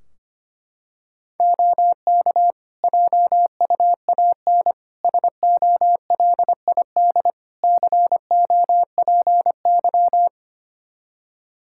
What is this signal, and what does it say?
Morse code, heard as a keyed tone, about 25 wpm, 700 Hz. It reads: OK JUAN SOLID COPY
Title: Practice QSO - Exchange 3 - More Info - 25wpm